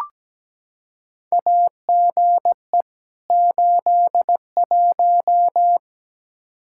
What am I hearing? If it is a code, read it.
Morse code, heard as a keyed tone, about 17 wpm, 700 Hz. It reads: AGE 81
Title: Practice QSO - Exchange 3 - More Info - 17wpm